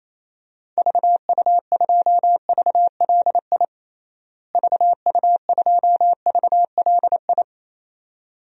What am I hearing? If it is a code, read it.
Morse code, heard as a keyed tone, about 28 wpm, 700 Hz. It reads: VU2VLI VU2VLI
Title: DXCC-100 Call Signs - 6 Characters - Repeated 2x - 28wpm